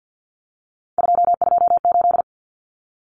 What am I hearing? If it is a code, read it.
Morse code, heard as a keyed tone, about 50 wpm, 700 Hz. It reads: 228
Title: Common Cut Numbers - 3-Digit 50wpm